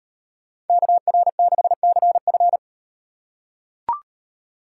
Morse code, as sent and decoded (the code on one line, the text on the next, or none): -.- .-. -.... -.-. ..-.
KR6CF